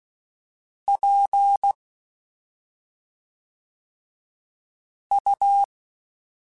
Morse code, heard as a keyed tone, about 16 wpm, 800 Hz. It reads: P U
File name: Special INGs / Special INGs-mp3s / SlowToFaster ing @5